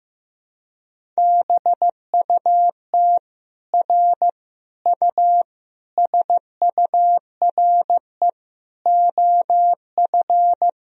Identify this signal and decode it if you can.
Morse code, heard as a keyed tone, about 15 wpm, 700 Hz. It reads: BUT R U SURE OF
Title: Sentences from Top 300 Words and Abbreviations 15wpm